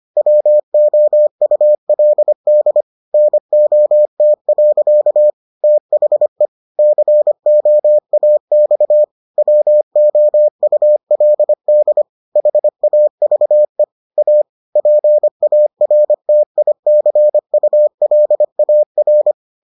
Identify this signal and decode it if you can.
Morse code, heard as a keyed tone, about 25 wpm, 600 Hz. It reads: WOULD NOT. THE COAX WOULD HAVE A PARTICULAR